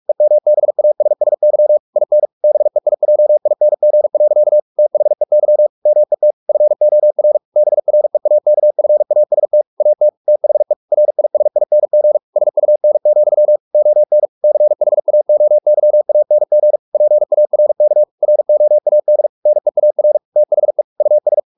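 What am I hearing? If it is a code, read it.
Morse code, heard as a keyed tone, about 44 wpm, 600 Hz. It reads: EMBASSY IN BEIJING. THEY MET FOR BREAKFAST AT THE RISING SUN, ON CHAOYANG PARK ROAD NEAR THE US
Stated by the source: Morse practice at 45 wpm, the novel Never by Ken Follett in English, from CW med Gnister.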